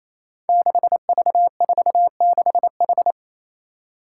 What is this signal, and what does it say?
Morse code, heard as a keyed tone, about 29 wpm, 700 Hz. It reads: 6V46H
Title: Five Challenging Letter-Number Combinations 28wpm